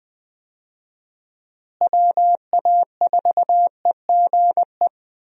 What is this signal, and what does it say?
Morse code, heard as a keyed tone, about 20 wpm, 700 Hz. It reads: WA4EGE
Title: USA Call Signs - 6 Characters - Repeated 3x - 20wpm